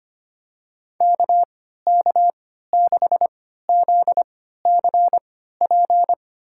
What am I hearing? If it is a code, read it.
Morse code, heard as a keyed tone, about 25 wpm, spaced as for 14 wpm, 700 Hz. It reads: KK6ZCP
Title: USA Call Signs - 3x Character Spacing - 25wpm